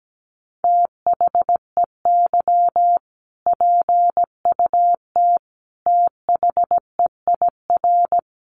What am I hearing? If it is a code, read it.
Morse code, heard as a keyed tone, about 17 wpm, 700 Hz. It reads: THEY PUT THEIR
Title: Sentences from Top 1200 Words 17wpm